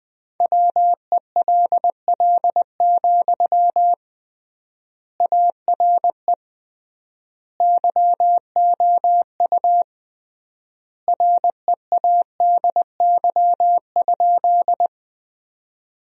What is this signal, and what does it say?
Morse code, heard as a keyed tone, about 19 wpm, 700 Hz. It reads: WELL, ARE YOU READY?
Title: 3x Word-Spacing - Sentences from Top 400 Words - 20wpm